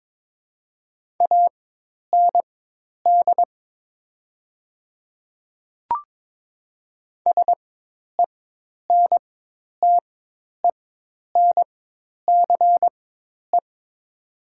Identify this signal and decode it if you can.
Morse code, heard as a keyed tone, about 21 wpm, spaced as for 10 wpm, 700 Hz. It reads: AND SENTENCE
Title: 4x Character Spacing - Top 200 Words - 22wpm